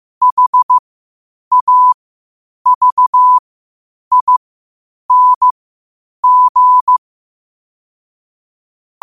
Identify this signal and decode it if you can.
Morse code, heard as a keyed tone, about 15 wpm, spaced as for 8 wpm, 1000 Hz. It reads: HAVING
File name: qotdfiles / 8wpm